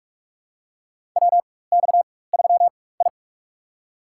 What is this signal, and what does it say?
Morse code, heard as a keyed tone, about 42 wpm, spaced as for 20 wpm, 700 Hz. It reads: WX3I
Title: USA Call Signs - 4x Character Spacing - 45wpm